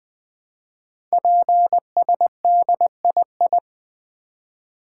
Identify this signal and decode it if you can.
Morse code, heard as a keyed tone, about 21 wpm, 700 Hz. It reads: PSDII
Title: Code Groups - 5 Letter:Numbers - 20wpm